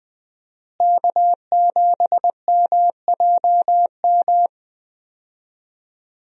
Morse code, heard as a keyed tone, about 20 wpm, 700 Hz. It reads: K7MJM
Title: USA Call Signs - 5 Characters - Repeated 3x - 20wpm